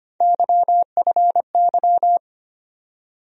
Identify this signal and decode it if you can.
Morse code, heard as a keyed tone, about 25 wpm, 700 Hz. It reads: YFY
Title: USA Call Sign Postfixes - 3 Letters 25wpm